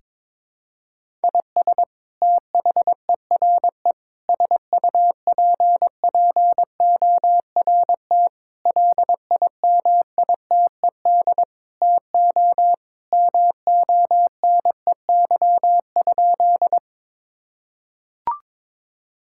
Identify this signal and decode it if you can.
Morse code, heard as a keyed tone, about 22 wpm, 700 Hz. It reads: IS THERE SUPPORT LIMITED TO MONEY?
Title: Sentences from Top 1000 Words - Review Part 6 - 22wpm